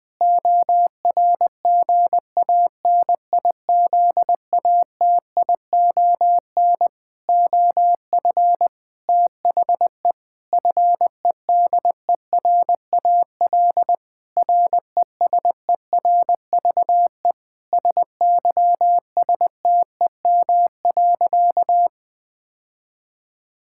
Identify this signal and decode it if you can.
Morse code, heard as a keyed tone, about 20 wpm, 700 Hz. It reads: ORGANIZATION OF THE FEDERAL RESERVE SYSTEM.
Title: Sentences from Top 1000 Words - Review Part 6 - 20wpm